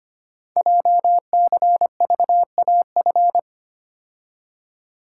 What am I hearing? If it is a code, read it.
Morse code, heard as a keyed tone, about 25 wpm, 700 Hz. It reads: JCVAF
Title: Code Groups - 5 Letters - 25wpm